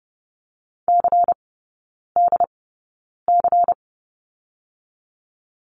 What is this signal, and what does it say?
Morse code, heard as a keyed tone, about 30 wpm, spaced as for 9 wpm, 700 Hz. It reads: CDC
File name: Lesson 16e - Letter c - Three Characters 30wpm